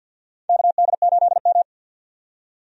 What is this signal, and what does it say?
Morse code, heard as a keyed tone, about 52 wpm, 700 Hz. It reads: KD8M
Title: USA Call Signs - 4 Characters - 50wpm